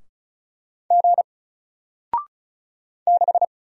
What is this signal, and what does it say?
Morse code, heard as a keyed tone, about 36 wpm, 700 Hz. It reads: G 6
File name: Single-Letter-Number-35wpm